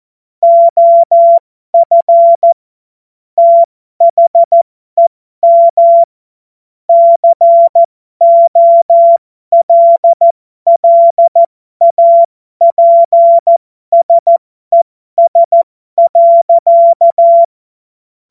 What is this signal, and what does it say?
Morse code, heard as a keyed tone, about 14 wpm, spaced as for 12 wpm, 680 Hz. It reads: OF THEM COLLAPSES.
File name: Morsekurs / Witze